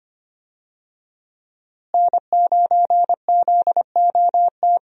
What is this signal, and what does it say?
Morse code, heard as a keyed tone, about 25 wpm, 700 Hz. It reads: N9ZOT
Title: USA Call Signs - 5 Characters - Repeated 2x - 25wpm